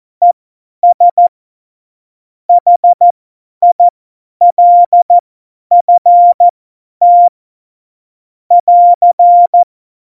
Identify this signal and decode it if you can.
Morse code, heard as a keyed tone, about 14 wpm, spaced as for 10 wpm, 700 Hz. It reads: ES HILFT <AR>
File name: Morsekurs / Witze / Witz 09. Januar 2018, 50 F70